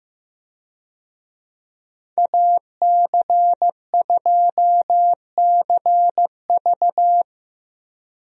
Morse code, heard as a keyed tone, about 16 wpm, 700 Hz. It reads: AC2CV
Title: USA Call Signs - 5 Characters - Repeated 3x - 15wpm